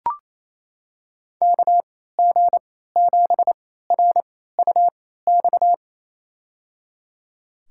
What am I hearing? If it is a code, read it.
Morse code, heard as a keyed tone, about 28 wpm, spaced as for 16 wpm, 700 Hz. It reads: KG7RUX
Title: USA Call Signs - 6 Characters - 3x Character Spacing - 28wpm